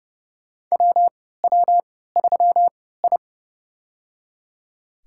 Morse code, heard as a keyed tone, about 32 wpm, spaced as for 17 wpm, 700 Hz. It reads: WW3I
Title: USA Call Signs - 3x Character Spacing - 30wpm